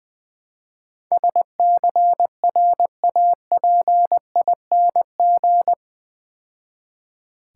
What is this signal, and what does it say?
Morse code, heard as a keyed tone, about 23 wpm, 700 Hz. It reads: SCRAPING
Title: Easy Words - Speed-Racing with Hints - 20wpm